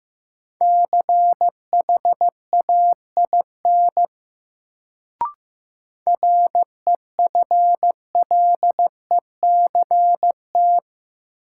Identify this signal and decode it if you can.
Morse code, heard as a keyed tone, about 15 wpm, 700 Hz. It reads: CHAIN REFLECT
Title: Sentences from Top 1500 Words - New Word Drill Alt - 15wpm